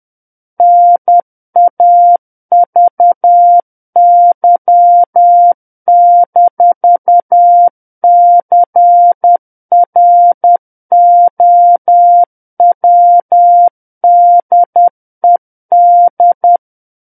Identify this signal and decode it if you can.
Morse code, heard as a keyed tone, about 10 wpm, 700 Hz. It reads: NAVY-CROWDED
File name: Book One / 01-Chapter 1 - 10 WPM